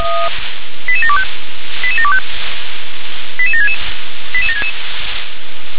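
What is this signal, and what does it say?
Five-tone selective call (ZVEI), heard as quick runs of five tones, tones of about 70 ms. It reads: 88725 88725 88766 88766
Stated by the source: ZVEI five-tone call 88560 received by SDR.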